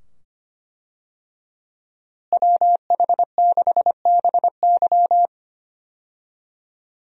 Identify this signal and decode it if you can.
Morse code, heard as a keyed tone, about 25 wpm, 700 Hz. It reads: WH6BY
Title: USA Call Signs - 5 Characters - Repeated 2x - 25wpm